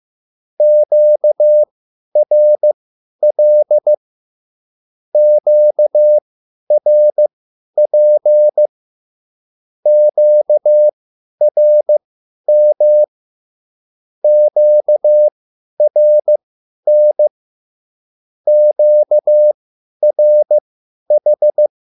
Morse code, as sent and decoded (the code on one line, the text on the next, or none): --.- .-. .-.. / --.- .-. .--. / --.- .-. -- / --.- .-. -. / --.- .-. ....
QRL QRP QRM QRN QRH